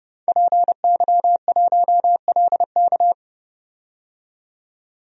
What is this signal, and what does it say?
Morse code, heard as a keyed tone, about 31 wpm, 700 Hz. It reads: PY1LK
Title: DXCC-100 Call Signs - 5 Characters - 30wpm